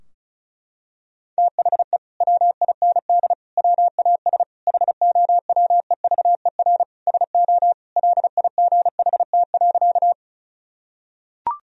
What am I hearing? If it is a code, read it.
Morse code, heard as a keyed tone, about 36 wpm, 700 Hz. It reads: THE WIND WAS HOWEVER SO LIGHT.
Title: Sentences from Top 1000 Words - Review Part 2 - 35wpm